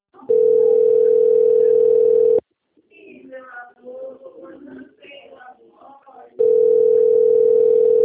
Ringback tone (North America), 440 + 480 Hz.
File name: out- - - - - .9656